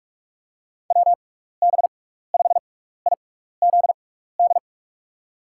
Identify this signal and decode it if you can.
Morse code, heard as a keyed tone, about 47 wpm, spaced as for 16 wpm, 700 Hz. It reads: WB5IZD
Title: USA Call Signs - 6x Character Spacing - 45wpm